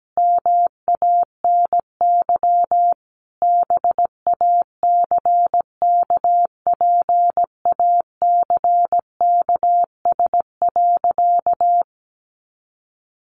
Morse code, heard as a keyed tone, about 17 wpm, 700 Hz. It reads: MANY BACKPACKS.
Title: Additional Easy Phrases 17wpm